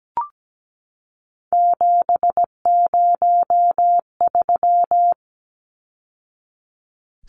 Morse code, heard as a keyed tone, about 17 wpm, 700 Hz. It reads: 703
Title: Numbers - 3-Digit 17wpm